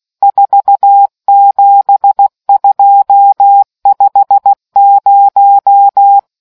Morse code, heard as a keyed tone, about 16 wpm, 800 Hz. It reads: 47250